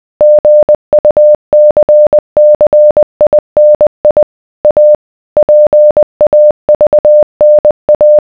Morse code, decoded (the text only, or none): GUCCINI A PAVNA